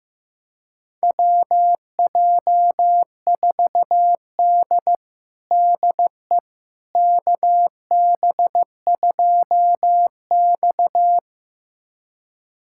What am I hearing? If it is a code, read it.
Morse code, heard as a keyed tone, about 15 wpm, 700 Hz. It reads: WJ4D DE KB2X
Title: USA Callsign DE Callsign - 4 Characters - 15wpm